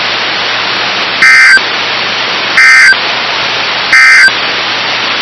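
Emergency Alert System (SAME) tones, heard as harsh buzzing bursts. It is heard from 1.1 to 4.4 s.
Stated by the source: NOAA Weather Radio station KIH54.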